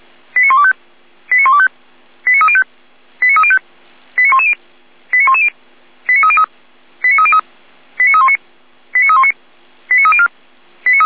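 Five-tone selective call (ZVEI), heard as quick runs of five tones, tones of about 70 ms. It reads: 89125 89125 89305 89305 89110 89110 89303 89303 89319 89319 89304
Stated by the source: ZVEI five-tone selective call 89042 received on an SDR.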